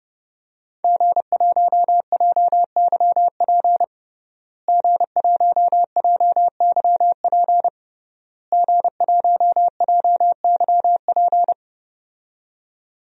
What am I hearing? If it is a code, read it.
Morse code, heard as a keyed tone, about 30 wpm, 700 Hz. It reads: G1JYP G1JYP G1JYP
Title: DXCC-100 Call Signs - 5 Characters - Repeated 3x - 30wpm